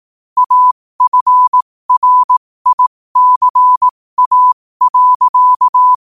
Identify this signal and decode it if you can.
Morse code, heard as a keyed tone, about 18 wpm, spaced as for 15 wpm, 1000 Hz. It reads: AFRICA.
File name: qotdfiles / 15wpm